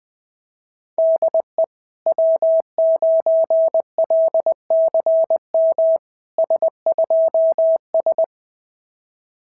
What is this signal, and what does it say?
Morse code, heard as a keyed tone, about 20 wpm, 660 Hz. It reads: DE W9LCM S2S
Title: SOTA QSO Practice 20wpm